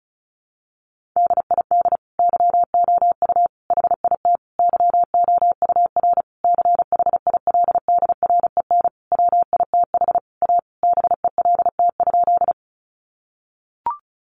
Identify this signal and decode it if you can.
Morse code, heard as a keyed tone, about 34 wpm, 700 Hz. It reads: DID YOU HIT YOUR CHILDREN WITH A BELT?
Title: Sentences from Top 1400 Words 35wpm